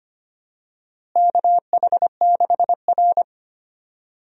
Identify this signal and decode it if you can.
Morse code, heard as a keyed tone, about 25 wpm, 700 Hz. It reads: KH6R
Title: USA Call Signs - 4 Characters - Repeated 3x - 25wpm